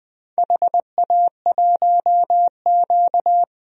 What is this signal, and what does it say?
Morse code, heard as a keyed tone, about 20 wpm, 700 Hz. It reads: HA1Q
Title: DXCC-100 Call Signs - 4 Characters - Repeated 3x - 20wpm